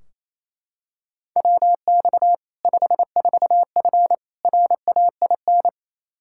Morse code, heard as a keyed tone, about 28 wpm, 700 Hz. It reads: WX 54F RAIN
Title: Practice QSO - Exchange 2 - Info - 28wpm